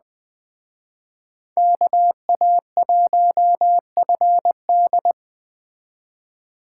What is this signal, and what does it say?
Morse code, heard as a keyed tone, about 20 wpm, 700 Hz. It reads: KA1FD